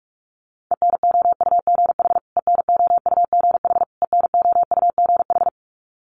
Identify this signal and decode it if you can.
Morse code, heard as a keyed tone, about 46 wpm, 700 Hz. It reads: ENOUGH ENOUGH ENOUGH
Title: Top 500 Words Repeated 45wpm